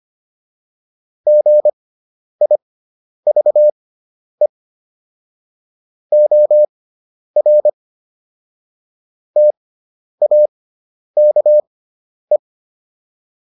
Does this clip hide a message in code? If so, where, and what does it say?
Yes: Morse code from 1.3 to 12.4 s, heard as a keyed tone, about 25 wpm, spaced as for 10 wpm, 600 Hz. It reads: GIVE OR TAKE